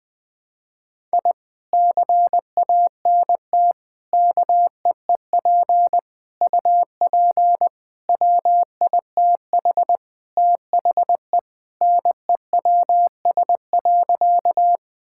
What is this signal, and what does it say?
Morse code, heard as a keyed tone, about 21 wpm, 700 Hz. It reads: I CANT KEEP UP WITH THE NEWS.
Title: Sentences with 2-Word Phrases - 20wpm